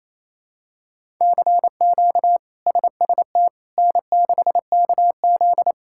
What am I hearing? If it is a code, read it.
Morse code, heard as a keyed tone, about 28 wpm, 700 Hz. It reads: CQ SST N6KZ
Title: K1USN SST QSO Practice - 28wpm